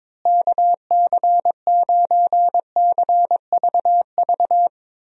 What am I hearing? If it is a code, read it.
Morse code, heard as a keyed tone, about 22 wpm, 700 Hz. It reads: KC9CVV
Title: USA Call Signs - 6 Characters - Repeated 3x - 22wpm